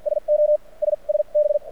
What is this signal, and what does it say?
Morse code, heard as a keyed tone, about 53 wpm, spaced as for 34 wpm, 600 Hz. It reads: SOS SB
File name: archived / CRSnet